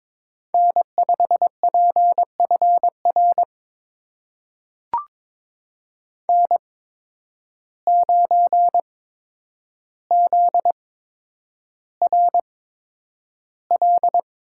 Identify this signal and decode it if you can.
Morse code, heard as a keyed tone, about 22 wpm, spaced as for 9 wpm, 700 Hz. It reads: N5PFR N9ZRL
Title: USA Call Signs - 5 Characters - 8x Character Spacing - 22wpm